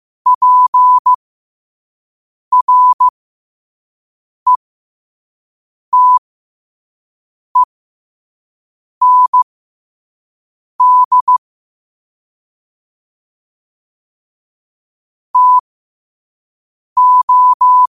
Morse code, heard as a keyed tone, about 15 wpm, spaced as for 5 wpm, 1000 Hz. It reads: PRETEND TO
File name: qotdfiles / 5wpm